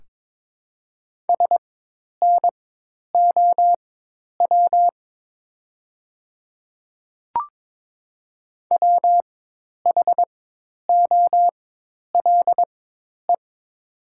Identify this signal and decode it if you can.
Morse code, heard as a keyed tone, about 21 wpm, spaced as for 10 wpm, 700 Hz. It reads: SNOW WHOLE
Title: 4x Character Spacing - Top 500 Words - 22wpm